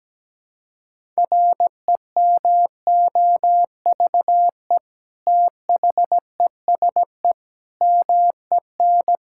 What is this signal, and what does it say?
Morse code, heard as a keyed tone, about 17 wpm, 700 Hz. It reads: REMOVE THESE MEN
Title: Sentences from Top 1000 Words - Review Part 5 - 17wpm